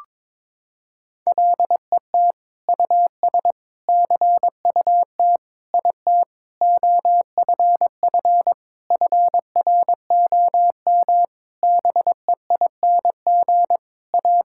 Morse code, heard as a keyed tone, about 22 wpm, 700 Hz. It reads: LET US CUT IT OFF FROM BEING A
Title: Sentences from Top 1000 Words - Review Part 6 - 22wpm